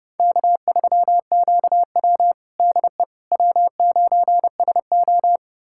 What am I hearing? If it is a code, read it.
Morse code, heard as a keyed tone, about 30 wpm, 700 Hz. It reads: K3QW DE W9SO
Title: USA Callsign DE Callsign - 4 Characters - 30wpm